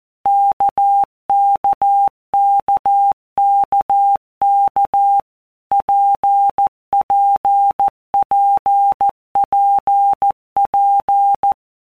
Morse code, heard as a keyed tone, about 14 wpm, 800 Hz. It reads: KKKKK PPPPP
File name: All Groups 1-minute Blocks / Groups@13 / RKPX